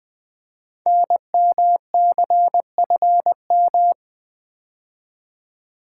Morse code, heard as a keyed tone, about 20 wpm, 700 Hz. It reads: NMCFM
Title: Code Groups - 5 Letters - 20wpm